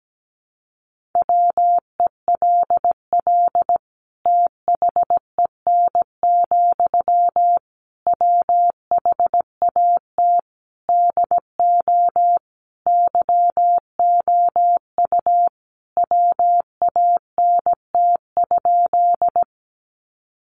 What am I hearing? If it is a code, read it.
Morse code, heard as a keyed tone, about 17 wpm, 700 Hz. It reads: WELL THEN, WHAT DO YOU WANT?
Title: Sentences from Top 200 Words 17wpm